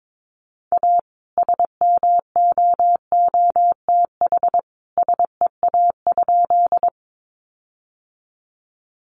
Morse code, heard as a keyed tone, about 22 wpm, 700 Hz. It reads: A SMOOTH SEA?
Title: Sentences from Top 1400 Words 22wpm